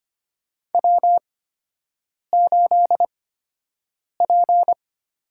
Morse code, heard as a keyed tone, about 25 wpm, spaced as for 7 wpm, 700 Hz. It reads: W8P
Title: USA Call Signs - 5 Characters - 8x Character Spacing - 25wpm